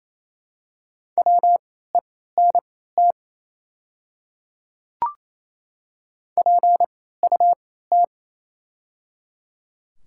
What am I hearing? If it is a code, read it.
Morse code, heard as a keyed tone, about 27 wpm, spaced as for 16 wpm, 700 Hz. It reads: WENT PUT
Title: 3x Character Spacing - Top 200 Words - 28wpm